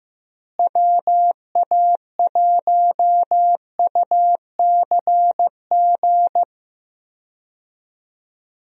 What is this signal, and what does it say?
Morse code, heard as a keyed tone, about 15 wpm, 700 Hz. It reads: WA1UCG
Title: USA Call Signs - 6 Characters - Repeated 2x - 15wpm